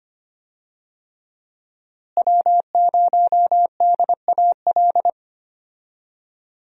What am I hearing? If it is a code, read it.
Morse code, heard as a keyed tone, about 25 wpm, 700 Hz. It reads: W0DAL
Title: USA Call Signs - 5 Characters - Repeated 3x - 25wpm